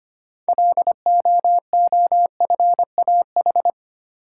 Morse code, heard as a keyed tone, about 25 wpm, 700 Hz. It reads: LOOFAH
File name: practice / 25wpm / Difficult Words with Hints - 25wpm